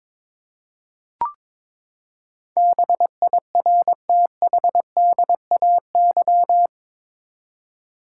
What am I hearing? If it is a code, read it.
Morse code, heard as a keyed tone, about 22 wpm, 700 Hz. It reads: BIRTHDAY
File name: practice / 22wpm / Eight Letter Words - 22wpm